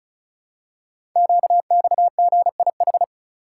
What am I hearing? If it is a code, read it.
Morse code, heard as a keyed tone, about 36 wpm, 690 Hz. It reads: QXGIH
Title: Code Groups - 5 Letters - 35wpm